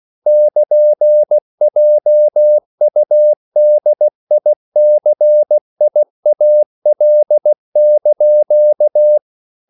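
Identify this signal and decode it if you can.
Morse code, heard as a keyed tone, about 16 wpm, 600 Hz. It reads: <KN>JUDICIAL)
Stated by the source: CW practice of national capitals at 16 words per minute, from ON6ZQ.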